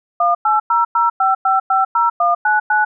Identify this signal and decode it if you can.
DTMF, heard as telephone touch tones, tones of about 180 ms. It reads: 18005550199
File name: dtmf / dtmf-dial